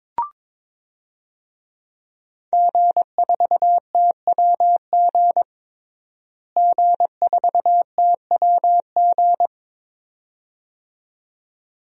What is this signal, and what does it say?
Morse code, heard as a keyed tone, about 22 wpm, 700 Hz. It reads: G4TWG G4TWG
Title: DXCC-100 Call Signs - 5 Characters - Repeated 2x - 22wpm